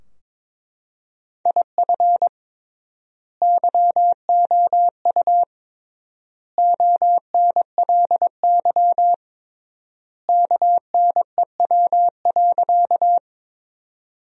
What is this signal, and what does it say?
Morse code, heard as a keyed tone, about 22 wpm, 700 Hz. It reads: IF YOU ONLY KNEW.
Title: 3x Word-Spacing - Sentences from Top 400 Words - 22wpm